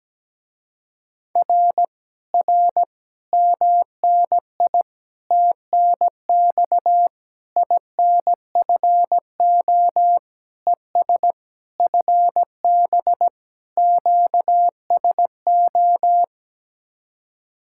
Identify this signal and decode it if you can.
Morse code, heard as a keyed tone, about 17 wpm, 700 Hz. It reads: R R MNI TNX INFO ES FB QSO
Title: Speed-Racing Common QSO Phrases - 17wpm